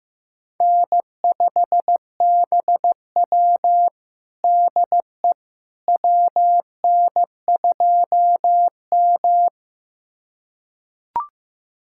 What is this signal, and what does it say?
Morse code, heard as a keyed tone, about 16 wpm, 700 Hz. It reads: N5BW DE WN2M
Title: USA Callsign DE Callsign - 4 Characters - 15wpm